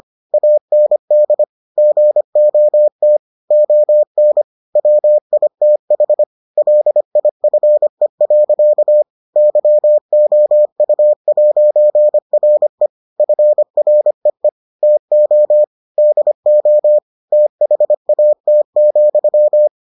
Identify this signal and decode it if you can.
Morse code, heard as a keyed tone, about 25 wpm, 600 Hz. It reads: AND GOT ON WITH LIFE. YOU'RE FREE TO DO THAT,